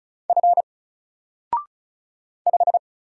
Morse code, heard as a keyed tone, about 36 wpm, 700 Hz. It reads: F 5
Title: Single Letter-Number Alt 35wpm